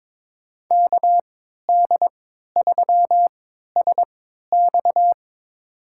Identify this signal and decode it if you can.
Morse code, heard as a keyed tone, about 21 wpm, spaced as for 12 wpm, 700 Hz. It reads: KD3SX
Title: USA Call Signs - 5 Characters - 3x Character Spacing - 22wpm